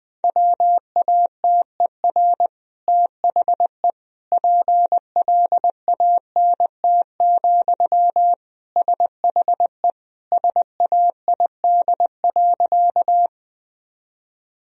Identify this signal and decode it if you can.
Morse code, heard as a keyed tone, about 20 wpm, 700 Hz. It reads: WATER THE PLANT, SHE SAID.
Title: Sentences from Top 1000 Words - Review Part 1 - 20wpm